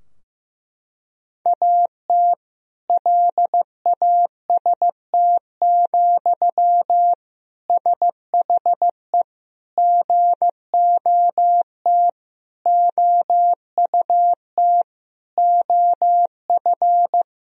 Morse code, heard as a keyed tone, about 15 wpm, 700 Hz. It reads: AT LAST, SHE GOT OUT OF
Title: Sentences from Top 1000 Words - Review Part 2 - 15wpm